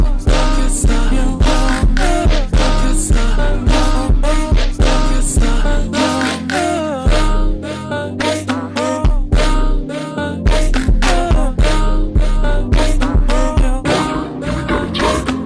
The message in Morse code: none